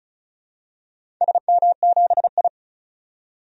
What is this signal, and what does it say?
Morse code, heard as a keyed tone, about 36 wpm, 700 Hz. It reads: SM7I
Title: DXCC-100 Call Signs - 4 Characters - 35wpm